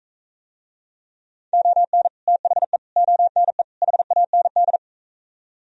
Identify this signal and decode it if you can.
Morse code, heard as a keyed tone, about 44 wpm, 700 Hz. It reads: ON THE ONE HAND